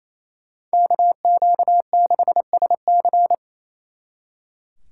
Morse code, heard as a keyed tone, about 37 wpm, 700 Hz. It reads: KQ6SC